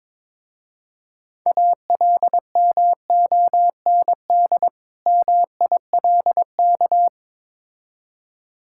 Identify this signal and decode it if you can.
Morse code, heard as a keyed tone, about 22 wpm, 700 Hz. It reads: ALMOND MILK
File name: Difficult Words with Hints - 22wpm